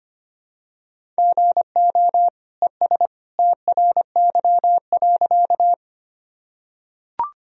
Morse code, heard as a keyed tone, about 25 wpm, 700 Hz. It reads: GO ES TRY.
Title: Sentences from Top 200 Words and Abbreviations 25wpm